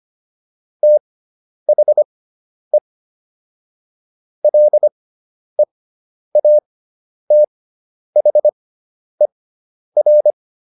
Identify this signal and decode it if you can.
Morse code, heard as a keyed tone, about 25 wpm, spaced as for 10 wpm, 600 Hz. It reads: THE LEATHER